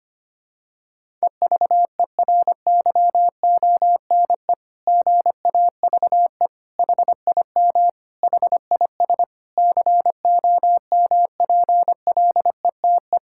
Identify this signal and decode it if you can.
Morse code, heard as a keyed tone, about 25 wpm, 700 Hz. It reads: EVERYONE GAVE HIM HIS COMPLETE
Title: Sentences from Top 1000 Words - Review Part 6 - 25wpm